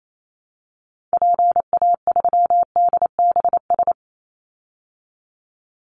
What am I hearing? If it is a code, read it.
Morse code, heard as a keyed tone, about 28 wpm, 700 Hz. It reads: PA3DBS
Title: Call Signs Repeated 3x - 28wpm